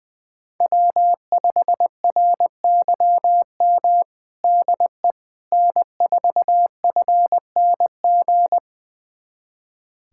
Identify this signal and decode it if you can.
Morse code, heard as a keyed tone, about 20 wpm, 700 Hz. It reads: W5RYM DE N4FNG
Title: USA Callsign DE Callsign - 5 Characters - 20wpm